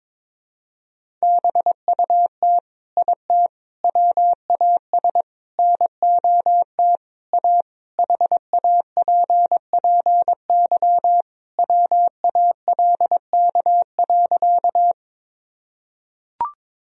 Morse code, heard as a keyed tone, about 22 wpm, 700 Hz. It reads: BUT IT WAS NOT A HAPPY WALK.
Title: Sentences from Top 1000 Words - Review Part 1 - 22wpm